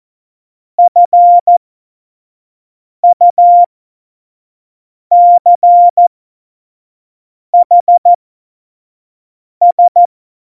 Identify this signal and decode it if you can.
Morse code, heard as a keyed tone, about 14 wpm, spaced as for 5 wpm, 700 Hz. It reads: FUCHS
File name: Morsekurs / Witze / Witz 23. Oktober 2018, 25 F70